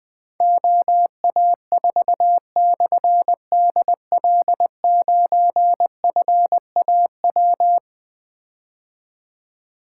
Morse code, heard as a keyed tone, about 20 wpm, 700 Hz. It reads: OA4/DL9FAW
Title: Difficult Call Signs 20wpm